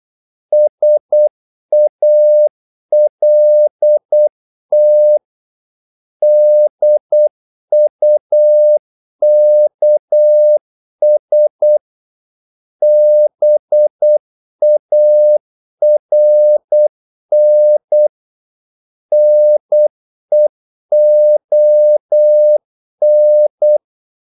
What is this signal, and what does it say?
Morse code, heard as a keyed tone, about 8 wpm, 600 Hz. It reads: SALT DUKS BARN NEON
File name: Korte ord DK 8wpm